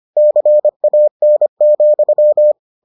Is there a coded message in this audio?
Yes: Morse code from 0.2 to 2.5 s, heard as a keyed tone, about 25 wpm, 600 Hz. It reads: CAN,